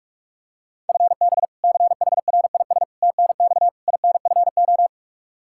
Morse code, heard as a keyed tone, about 46 wpm, 700 Hz. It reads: FB CHRIS TNX INFO
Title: Practice QSO - Exchange 3 - More Info - 45wpm